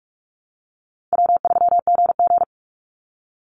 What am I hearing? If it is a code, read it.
Morse code, heard as a keyed tone, about 46 wpm, 700 Hz. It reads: W3GG
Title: Call Signs - 45wpm